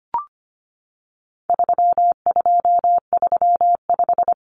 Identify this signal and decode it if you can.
Morse code, heard as a keyed tone, about 25 wpm, 700 Hz. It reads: 3235
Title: Numbers - 4-Digit 25wpm